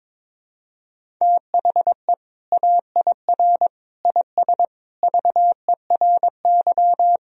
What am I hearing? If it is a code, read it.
Morse code, heard as a keyed tone, about 22 wpm, 700 Hz. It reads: THE AIR IS VERY
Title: Sentences from Top 1000 Words - Review Part 1 - 22wpm